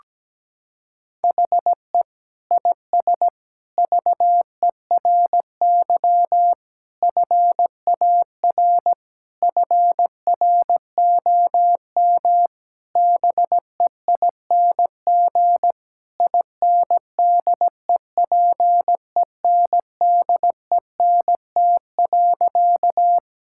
Morse code, heard as a keyed tone, about 17 wpm, 700 Hz. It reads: HE IS VERY FAR FROM BEING INDEPENDENT.
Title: Sentences from Top 1000 Words - Review Part 6 - 17wpm